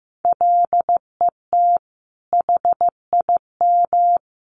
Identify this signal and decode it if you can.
Morse code, heard as a keyed tone, about 15 wpm, 700 Hz. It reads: LET HIM
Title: Sentences from Top 1100 Words 15wpm